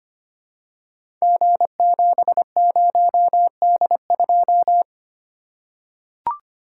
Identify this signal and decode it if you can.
Morse code, heard as a keyed tone, about 25 wpm, 700 Hz. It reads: G70D2
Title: Code Groups - 5 Letter:Numbers - 25wpm